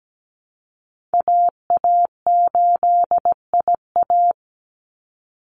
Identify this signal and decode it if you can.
Morse code, heard as a keyed tone, about 17 wpm, 700 Hz. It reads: AA8IA
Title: Call Signs Repeated 3x - 17wpm